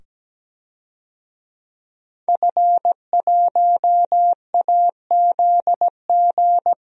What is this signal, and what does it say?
Morse code, heard as a keyed tone, about 17 wpm, 700 Hz. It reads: F1AZG
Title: DXCC-100 Call Signs - 5 Characters - 17wpm